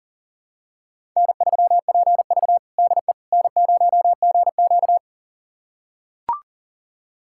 Morse code, heard as a keyed tone, about 40 wpm, 700 Hz. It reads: N3PV DE N0GQ
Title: USA Callsign DE Callsign - 4 Characters - 40wpm